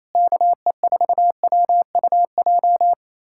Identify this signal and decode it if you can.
Morse code, heard as a keyed tone, about 27 wpm, 700 Hz. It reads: KE4WUJ
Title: USA Callsign DE Callsign 28wpm